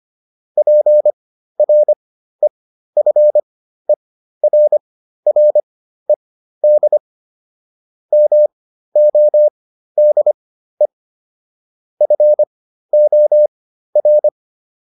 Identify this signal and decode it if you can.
Morse code, heard as a keyed tone, about 25 wpm, spaced as for 13 wpm, 600 Hz. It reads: PREFERRED MODE FOR